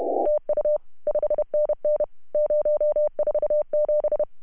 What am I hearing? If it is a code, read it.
Morse code, heard as a keyed tone, about 28 wpm, 590 Hz. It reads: TU 5NN 047
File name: CW / HA2OS RHOB 2017 CW